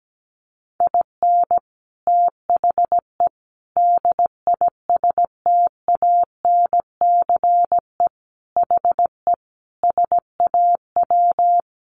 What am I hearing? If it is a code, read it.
Morse code, heard as a keyed tone, about 17 wpm, 700 Hz. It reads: IN THE DISTANCE HE SAW
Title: Sentences from Top 1200 Words 17wpm